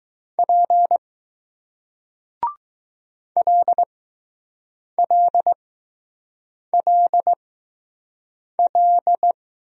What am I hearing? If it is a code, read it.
Morse code, heard as a keyed tone, about 20 wpm, spaced as for 6 wpm, 700 Hz. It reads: P LLLL